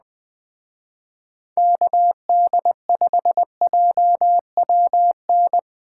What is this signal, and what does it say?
Morse code, heard as a keyed tone, about 20 wpm, 700 Hz. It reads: KD5JWN
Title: Speed Racing - USA Call Signs 17wpm